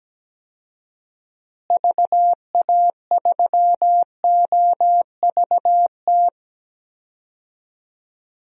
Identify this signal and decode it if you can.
Morse code, heard as a keyed tone, about 17 wpm, 700 Hz. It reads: VA3OVT
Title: DXCC-100 Call Signs - 6 Characters - Repeated 3x - 17wpm